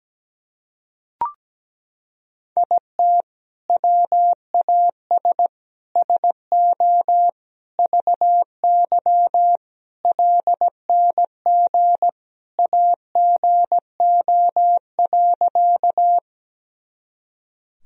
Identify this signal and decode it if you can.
Morse code, heard as a keyed tone, about 17 wpm, 700 Hz. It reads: IT WAS SO VY LNG AGO.
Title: Sentences from Top 500 Words and Abbreviations 17wpm